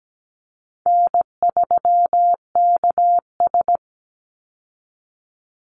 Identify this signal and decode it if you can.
Morse code, heard as a keyed tone, about 17 wpm, 700 Hz. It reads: N3KS
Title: Easy Call Signs Repeated 2x - 17wpm